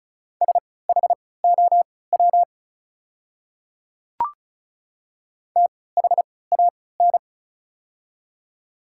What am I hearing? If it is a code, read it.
Morse code, heard as a keyed tone, about 34 wpm, spaced as for 20 wpm, 700 Hz. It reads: SHOW THAN
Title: 3x Character Spacing - Top 200 Words - 35wpm